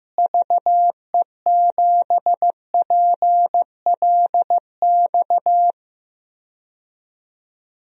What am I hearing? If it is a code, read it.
Morse code, heard as a keyed tone, about 15 wpm, 700 Hz. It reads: VE7PLX
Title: DXCC-100 Call Signs - 6 Characters - Repeated 2x - 15wpm